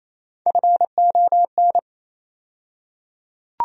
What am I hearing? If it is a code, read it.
Morse code, heard as a keyed tone, about 28 wpm, 700 Hz. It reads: FON
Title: USA Call Sign Postfixes - 3 Letters 28wpm